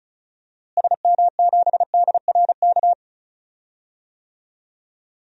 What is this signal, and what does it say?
Morse code, heard as a keyed tone, about 36 wpm, 700 Hz. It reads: SM7DRK
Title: DXCC-100 Call Signs - 6 Characters - Repeated 2x - 35wpm